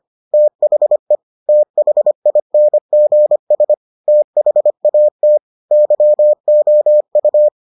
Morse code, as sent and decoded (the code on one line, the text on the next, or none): - .... / - .... .. -. --. ... / - .... .- - / -.-- --- ..-
THE THINGS THAT YOU